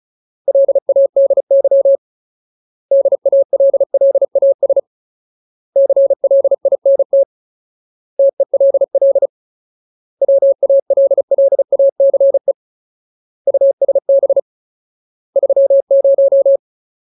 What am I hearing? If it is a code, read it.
Morse code, heard as a keyed tone, about 36 wpm, 550 Hz. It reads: LADY DALLAS CLINT TELL WALLACE USB 30
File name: Ultimate Word List / 35WPM Ultimate